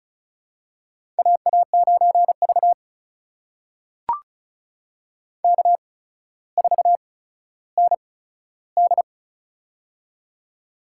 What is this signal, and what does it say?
Morse code, heard as a keyed tone, about 36 wpm, spaced as for 15 wpm, 700 Hz. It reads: AA9V K4ND
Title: USA Call Signs - 4 Characters - 8x Character Spacing - 35wpm